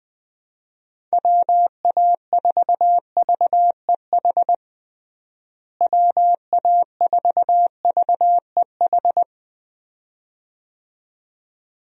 Morse code, heard as a keyed tone, about 20 wpm, 700 Hz. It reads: WA4VEH WA4VEH
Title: USA Call Signs - 6 Characters - Repeated 2x - 20wpm